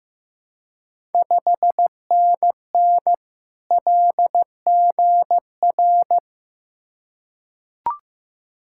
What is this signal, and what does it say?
Morse code, heard as a keyed tone, about 15 wpm, 700 Hz. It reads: 5NN LGR
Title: Ontario QSO Party Exchange 15wpm